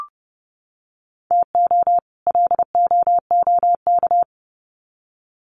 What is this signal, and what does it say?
Morse code, heard as a keyed tone, about 30 wpm, 700 Hz. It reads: TO LOOK
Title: Sets of 2 Words 30wpm